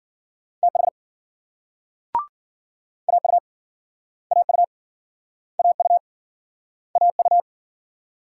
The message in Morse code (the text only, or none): TH AU AU AU AU